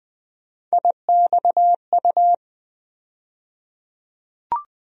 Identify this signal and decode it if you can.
Morse code, heard as a keyed tone, about 20 wpm, 700 Hz. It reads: IXU
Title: USA Call Sign Postfixes - 3 Letters 20wpm